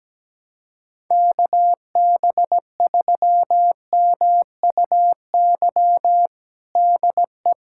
Morse code, heard as a keyed tone, about 17 wpm, 700 Hz. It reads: KB3MUY DE
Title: USA Callsign DE Callsign - 6 Characters - 17wpm